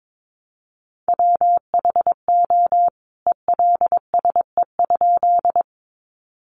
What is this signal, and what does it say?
Morse code, heard as a keyed tone, about 22 wpm, 700 Hz. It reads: WHO ELSE?
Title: A Sound Of Thunder 22wpm